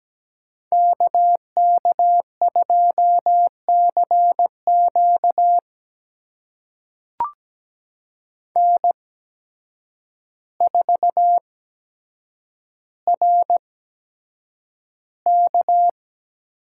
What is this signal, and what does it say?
Morse code, heard as a keyed tone, about 17 wpm, spaced as for 8 wpm, 700 Hz. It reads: KK2CQ N4RK
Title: USA Call Signs - 5 Characters - 8x Character Spacing - 17wpm